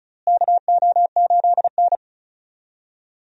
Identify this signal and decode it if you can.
Morse code, heard as a keyed tone, about 36 wpm, 700 Hz. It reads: KO8N
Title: USA Call Signs - 4 Characters - Repeated 3x - 35wpm